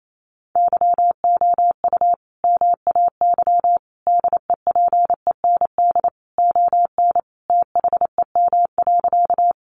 Morse code, heard as a keyed tone, about 28 wpm, 700 Hz. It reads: YOU MAY DEPEND ON THEM.
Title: Sentences from Top 1300 Words 28wpm